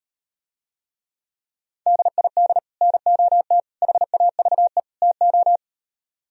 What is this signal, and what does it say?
Morse code, heard as a keyed tone, about 38 wpm, 700 Hz. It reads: DID NOT HAVE TO